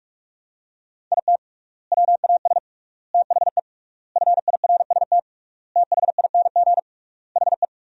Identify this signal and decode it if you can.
Morse code, heard as a keyed tone, about 46 wpm, 700 Hz. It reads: IT WAS THE FIRST THING HE
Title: 3x Word-Spacing - Sentences from Top 300 Words - 45wpm